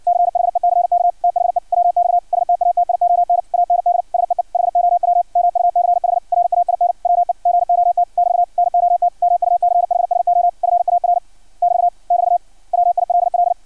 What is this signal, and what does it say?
Morse code, heard as a keyed tone, about 56 wpm, 700 Hz. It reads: CHECK THE WX STATION AND SEE HOW MUCH RAIN WE GOT <BT> NOT MUCH SO FAR <BT> <BT> WILL
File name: archived / CRSnet